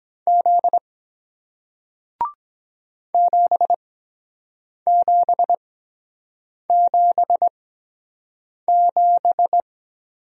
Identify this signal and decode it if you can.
Morse code, heard as a keyed tone, about 23 wpm, spaced as for 7 wpm, 700 Hz. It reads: Z 7777